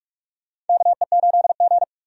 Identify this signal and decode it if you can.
Morse code, heard as a keyed tone, about 43 wpm, 700 Hz. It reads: KE8G
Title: USA Call Signs - 4 Characters - 45wpm